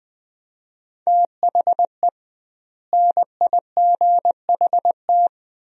Morse code, heard as a keyed tone, about 21 wpm, 700 Hz. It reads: THE NIGHT
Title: 2x Word-Spacing - Sentences from Top 500 Words - 20wpm